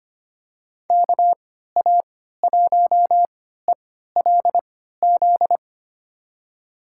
Morse code, heard as a keyed tone, about 25 wpm, spaced as for 14 wpm, 700 Hz. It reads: KA1ELZ
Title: USA Call Signs - 6 Characters - 3x Character Spacing - 25wpm